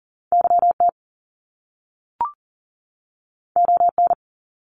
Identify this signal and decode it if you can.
Morse code, heard as a keyed tone, about 40 wpm, 700 Hz. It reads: YT ON
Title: Top Two Letter Combinations in Words 40wpm